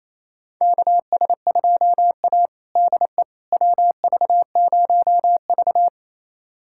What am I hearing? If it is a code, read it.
Morse code, heard as a keyed tone, about 28 wpm, 700 Hz. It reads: KS2A DE WV0V
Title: USA Callsign DE Callsign - 4 Characters - 28wpm